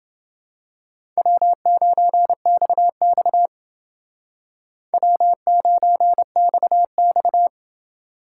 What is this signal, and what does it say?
Morse code, heard as a keyed tone, about 29 wpm, 700 Hz. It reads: W9XX W9XX